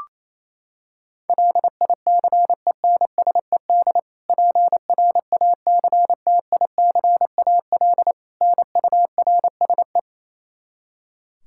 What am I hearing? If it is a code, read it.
Morse code, heard as a keyed tone, about 28 wpm, 700 Hz. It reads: LICENSED PRACTICAL NURSE
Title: Practice QSO - Exchange 3 - More Info - 28wpm